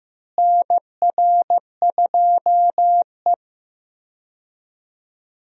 Morse code, heard as a keyed tone, about 16 wpm, 700 Hz. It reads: NR2E
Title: USA Call Signs - 4 Characters - Repeated 3x - 15wpm